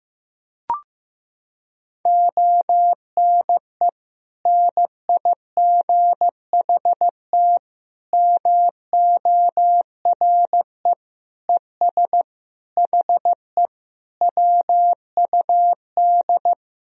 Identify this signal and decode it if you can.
Morse code, heard as a keyed tone, about 15 wpm, 700 Hz. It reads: ONE NIGHT MORE ES HE WUD
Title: Sentences from Top 300 Words and Abbreviations 15wpm